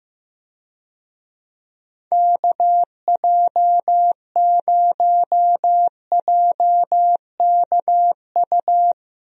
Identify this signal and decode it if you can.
Morse code, heard as a keyed tone, about 15 wpm, 700 Hz. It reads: KJ0JKU
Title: USA Call Signs - 6 Characters - Repeated 2x - 15wpm